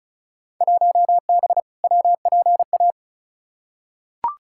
Morse code, heard as a keyed tone, about 36 wpm, 700 Hz. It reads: 1B WPA
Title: Field Day Exchanges - 35wpm